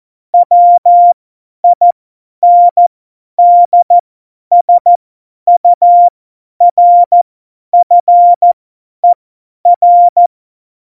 Morse code, heard as a keyed tone, about 14 wpm, spaced as for 10 wpm, 700 Hz. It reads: WINDSURFER
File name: Morsekurs / Witze / Witz 26. Juni 2018, 50 F70